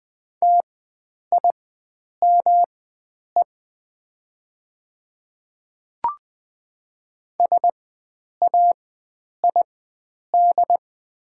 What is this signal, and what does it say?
Morse code, heard as a keyed tone, about 20 wpm, spaced as for 9 wpm, 700 Hz. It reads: TIME SAID
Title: 4x Character Spacing - Top 100 Words - 20wpm